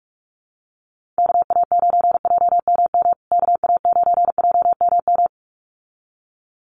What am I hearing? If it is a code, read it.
Morse code, heard as a keyed tone, about 44 wpm, 690 Hz. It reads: KA9JMM KA9JMM
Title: USA Call Signs Repeated 2x 45wpm